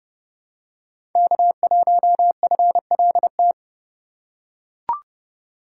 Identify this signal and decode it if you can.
Morse code, heard as a keyed tone, about 30 wpm, 700 Hz. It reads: K1FLT
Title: USA Call Signs - 5 Characters - 3x Character Spacing - 30wpm